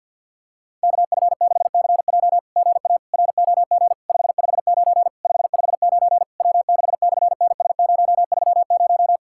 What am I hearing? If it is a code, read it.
Morse code, heard as a keyed tone, about 51 wpm, 700 Hz. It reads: KF6CJ GA ROG 559 559 W6/NS030
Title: SOTA QSO Practice 50wpm